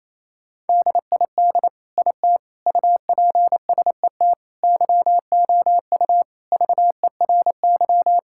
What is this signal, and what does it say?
Morse code, heard as a keyed tone, about 28 wpm, 700 Hz. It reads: DID IT UPSET YOU VERY
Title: Sentences from Top 1000 Words - Review Part 5 - 28wpm